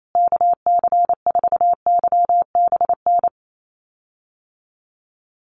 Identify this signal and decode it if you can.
Morse code, heard as a keyed tone, about 28 wpm, 700 Hz. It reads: KC4YBN
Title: USA Call Signs Repeated 3x 28wpm